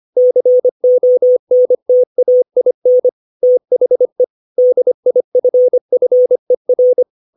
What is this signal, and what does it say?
Morse code, heard as a keyed tone, about 25 wpm, 500 Hz. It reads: CONTAIN THE DIFFER
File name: 632 words / 632 Words 25WPM